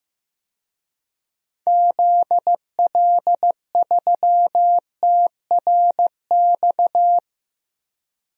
Morse code, heard as a keyed tone, about 15 wpm, 700 Hz. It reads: ZL3TRX
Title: DXCC-100 Call Signs - 6 Characters - Repeated 3x - 15wpm